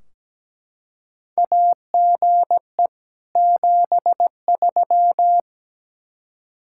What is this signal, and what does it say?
Morse code, heard as a keyed tone, about 17 wpm, 700 Hz. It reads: AGE 73
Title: Practice QSO - Exchange 3 - More Info - 17wpm